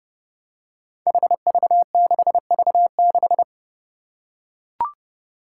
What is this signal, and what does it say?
Morse code, heard as a keyed tone, about 30 wpm, 700 Hz. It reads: HV6V6
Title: Five Challenging Letter-Number Combinations 30wpm